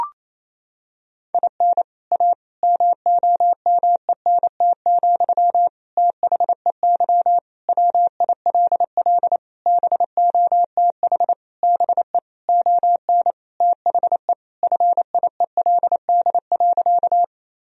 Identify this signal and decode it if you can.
Morse code, heard as a keyed tone, about 28 wpm, 700 Hz. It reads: IN A MOMENT, THEY WILL BOTH BE ON THE FIELD.
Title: Sentences from Top 1000 Words - Review Part 2 - 28wpm